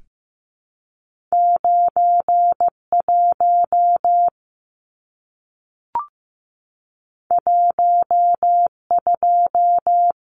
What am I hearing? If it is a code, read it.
Morse code, heard as a keyed tone, about 15 wpm, 700 Hz. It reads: 91 12
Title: Numbers - 2-Digit 15wpm